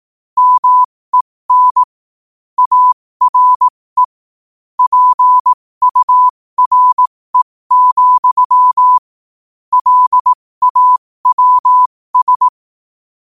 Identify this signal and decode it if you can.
Morse code, heard as a keyed tone, about 18 wpm, spaced as for 15 wpm, 1000 Hz. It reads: MEN ARE PURE, LAWS
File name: qotdfiles / 15wpm